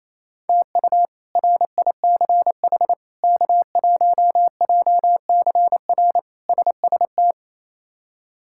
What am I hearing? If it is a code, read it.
Morse code, heard as a keyed tone, about 28 wpm, 700 Hz. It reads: TU RICH K1JCR SST
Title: K1USN SST QSO Practice - 28wpm